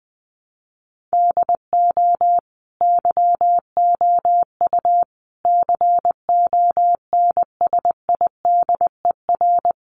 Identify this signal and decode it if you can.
Morse code, heard as a keyed tone, about 20 wpm, 700 Hz. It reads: DO YOU CONSIDER
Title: Sentences from Top 600 Words 20wpm